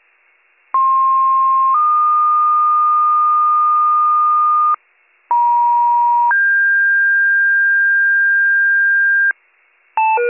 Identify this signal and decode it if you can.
Two-tone page, heard as a one-second tone, then a three-second tone, about 1060 then 1220 Hz, about 950 then 1600 Hz, with noise. It is heard from 0.7 to 9.3 s.